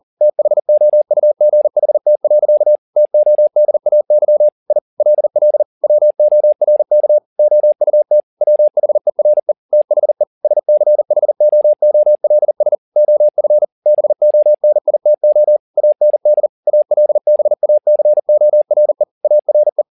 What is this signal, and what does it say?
Morse code, heard as a keyed tone, about 40 wpm, 600 Hz. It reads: THOUGHT. TODAY I LL WORK OUT WHERE THE SCHOOLS OF BONITO AND ALBACORE ARE